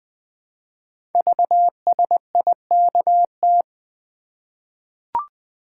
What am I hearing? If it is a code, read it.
Morse code, heard as a keyed tone, about 20 wpm, 700 Hz. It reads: VSIKT
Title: Code Groups - 5 Letters - 20wpm